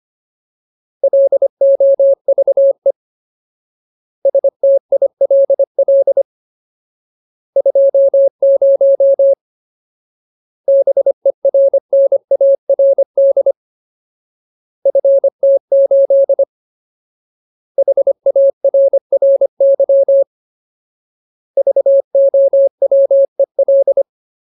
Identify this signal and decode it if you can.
Morse code, heard as a keyed tone, about 25 wpm, 550 Hz. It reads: LOVE STILL 20 BERNARD FT8 HARRY VOWEL
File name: Ultimate Word List / 25WPM Ultimate